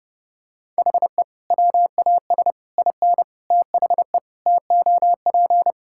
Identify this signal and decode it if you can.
Morse code, heard as a keyed tone, about 30 wpm, 700 Hz. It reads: HE WAS IN THE TOP
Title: Sentences from Top 1600 Words 30wpm